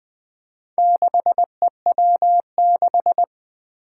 Morse code, heard as a keyed tone, about 20 wpm, 700 Hz. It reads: 6EW6
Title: Four Challenging Letter-Number Combinations 20wpm